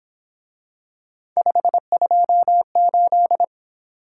Morse code, heard as a keyed tone, about 27 wpm, 700 Hz. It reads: 528
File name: Speed-Racing - Three Letter-Number Combinations - 22wpm